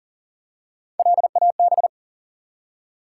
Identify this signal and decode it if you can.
Morse code, heard as a keyed tone, about 38 wpm, 700 Hz. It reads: LAB
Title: Three Letter Words - 40wpm